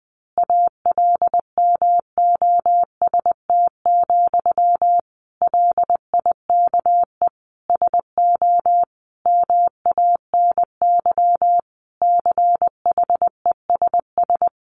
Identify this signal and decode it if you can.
Morse code, heard as a keyed tone, about 20 wpm, 700 Hz. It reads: ALMOST, LIKE SO MANY CHESS
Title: A Sound Of Thunder 20wpm